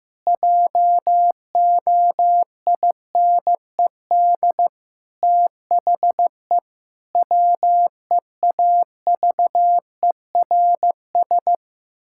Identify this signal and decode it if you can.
Morse code, heard as a keyed tone, about 16 wpm, 700 Hz. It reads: JOINED THE WEAVERS
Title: Pangram Challenge v2 - 15wpm